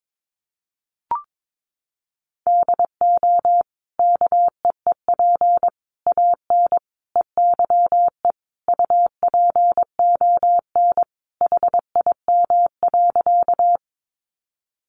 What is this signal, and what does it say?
Morse code, heard as a keyed tone, about 22 wpm, 700 Hz. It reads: DO KEEP AN EYE UPON HIM.
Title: Sentences from Top 1100 Words 22wpm